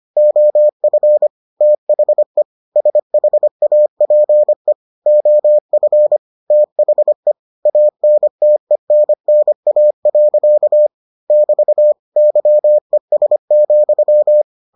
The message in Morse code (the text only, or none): OF THE SHAPE OF THE ANTENNA. <BT> YES,